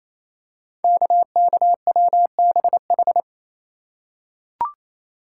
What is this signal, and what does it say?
Morse code, heard as a keyed tone, about 28 wpm, 700 Hz. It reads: KKWBH
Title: Code Groups - 5 Letters - 28wpm